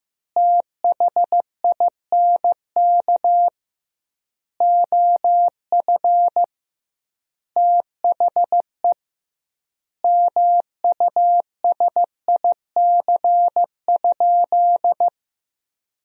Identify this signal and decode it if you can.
Morse code, heard as a keyed tone, about 15 wpm, 700 Hz. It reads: THINK OF THE MUSIC?
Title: 2x Word-Spacing - Sentences from Top 300 Words - 15wpm